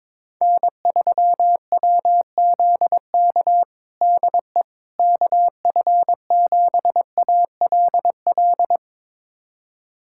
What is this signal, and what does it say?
Morse code, heard as a keyed tone, about 22 wpm, 700 Hz. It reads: N3WZK DE KF7ALL
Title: USA Callsign DE Callsign 22wpm